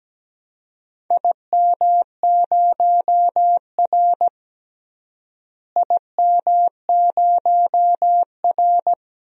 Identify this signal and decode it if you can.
Morse code, heard as a keyed tone, about 17 wpm, 700 Hz. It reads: IM0R IM0R
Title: DXCC-100 Call Signs - 4 Characters - Repeated 3x - 17wpm